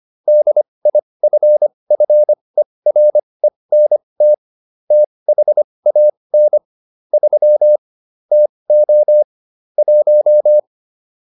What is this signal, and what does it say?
Morse code, heard as a keyed tone, about 25 wpm, spaced as for 20 wpm, 600 Hz. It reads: DIFFERENT THAN 3 TO 1